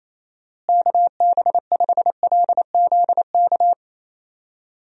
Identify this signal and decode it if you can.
Morse code, heard as a keyed tone, about 28 wpm, 700 Hz. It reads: KB5LZK
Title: USA Call Signs - 6 Characters - Repeated 2x - 28wpm